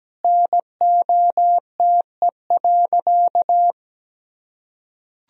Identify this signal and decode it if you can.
Morse code, heard as a keyed tone, about 17 wpm, 700 Hz. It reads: NOTE.
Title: Sentences from Top 1000 Words - Review Part 2 - 17wpm